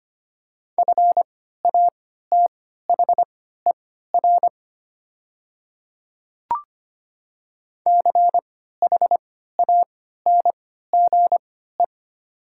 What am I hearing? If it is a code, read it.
Morse code, heard as a keyed tone, about 25 wpm, spaced as for 14 wpm, 690 Hz. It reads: FATHER CHANGE
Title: 3x Character Spacing - Top 200 Words - 25wpm